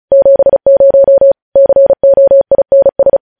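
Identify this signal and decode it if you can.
Morse code, heard as a keyed tone, about 36 wpm, 550 Hz. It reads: 70 COINS
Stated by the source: Morse news headlines practice at 35.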